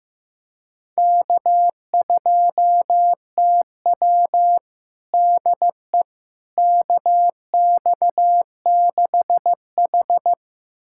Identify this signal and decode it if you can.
Morse code, heard as a keyed tone, about 15 wpm, 700 Hz. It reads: K2TW DE KX6H
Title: USA Callsign DE Callsign - 4 Characters - 15wpm